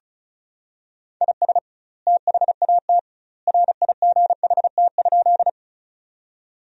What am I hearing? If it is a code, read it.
Morse code, heard as a keyed tone, about 34 wpm, 700 Hz. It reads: IS THAT RIGHT?
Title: 2x Word-Spacing - Sentences from Top 200 Words - 35wpm